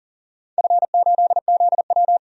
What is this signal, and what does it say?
Morse code, heard as a keyed tone, about 40 wpm, 700 Hz. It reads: F8ZW
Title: DXCC-100 Call Signs - 4 Characters - 40wpm